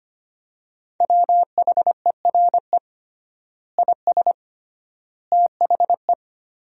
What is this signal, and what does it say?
Morse code, heard as a keyed tone, about 25 wpm, 700 Hz. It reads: WHERE IS THE
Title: 3x Word-Spacing - Sentences from Top 300 Words - 25wpm